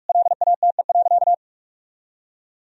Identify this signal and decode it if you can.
Morse code, heard as a keyed tone, about 45 wpm, 700 Hz. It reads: LATE.
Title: Sentences from Top 1000 Words - Review Part 3 - 45wpm